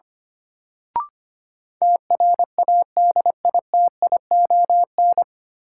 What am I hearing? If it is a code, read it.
Morse code, heard as a keyed tone, about 25 wpm, 700 Hz. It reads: TRADITION
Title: Sentences from Top 1500 Words - New Word Drill Alt - 25wpm